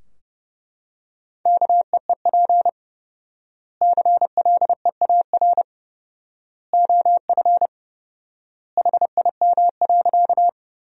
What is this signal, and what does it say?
Morse code, heard as a keyed tone, about 30 wpm, 700 Hz. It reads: KEEP CLEAR OF HIM.
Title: 4x Word-Spacing - Sentences from Top 500 Words - 30wpm